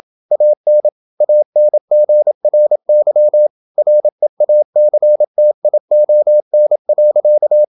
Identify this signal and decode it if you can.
Morse code, heard as a keyed tone, about 27 wpm, 600 Hz. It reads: AN ANGRY REACTION.
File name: Never 0027wpm